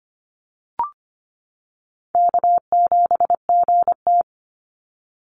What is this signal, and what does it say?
Morse code, heard as a keyed tone, about 25 wpm, 700 Hz. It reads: K7GT
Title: NA SOTA Chaser Callsigns 25wpm